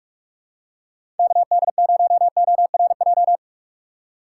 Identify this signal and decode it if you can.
Morse code, heard as a keyed tone, about 46 wpm, 700 Hz. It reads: KD0ORJ
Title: USA Call Signs - 6 Characters - 4x Character Spacing - 45wpm